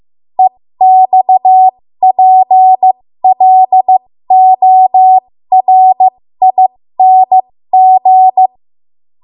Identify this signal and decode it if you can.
Morse code, heard as a keyed tone, about 15 wpm, spaced as for 13 wpm, 750 Hz. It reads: EXPLORING
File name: Archive / 13 WPM / 13WPM